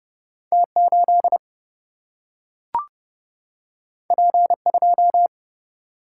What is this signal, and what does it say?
Morse code, heard as a keyed tone, about 30 wpm, 700 Hz. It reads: T8 P2
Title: Single-Letter Single-Number 30wpm